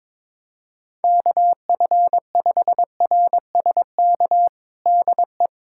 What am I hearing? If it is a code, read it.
Morse code, heard as a keyed tone, about 22 wpm, 700 Hz. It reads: KF5RSK DE
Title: USA Callsign DE Callsign - 6 Characters - 22wpm